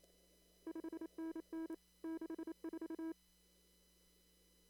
Morse code, heard as a keyed tone, about 28 wpm, 320 Hz. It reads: 5NN 64
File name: CW / kt5j wpx cw-op-k5pi